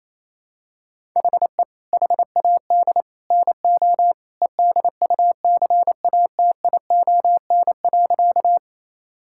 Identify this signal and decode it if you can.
Morse code, heard as a keyed tone, about 28 wpm, 700 Hz. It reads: HE HAD NO EDUCATION.
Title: Sentences from Top 1000 Words - Review Part 6 - 28wpm